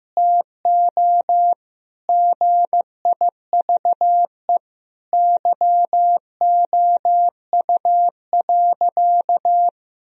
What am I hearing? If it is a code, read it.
Morse code, heard as a keyed tone, about 15 wpm, 700 Hz. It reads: TO GIVE YOU.
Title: Sentences from Top 1000 Words - Review Part 2 - 15wpm